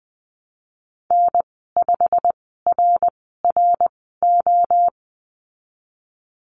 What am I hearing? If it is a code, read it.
Morse code, heard as a keyed tone, about 19 wpm, spaced as for 14 wpm, 700 Hz. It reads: N5RRO
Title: 173 Lesson 28g - Number 9 - Call Signs 20wpm